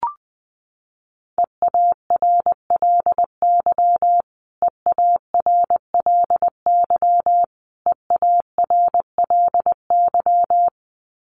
Morse code, heard as a keyed tone, about 20 wpm, 700 Hz. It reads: EARLY EARLY EARLY
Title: Top 500 Words Repeated 20wpm